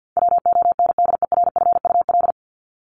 Morse code, heard as a keyed tone, about 50 wpm, 700 Hz. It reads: WONDERFUL
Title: Top 1001-1250 Words 50wpm